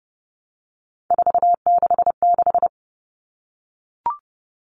Morse code, heard as a keyed tone, about 29 wpm, 700 Hz. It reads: 466
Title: Numbers - 3-Digit 30wpm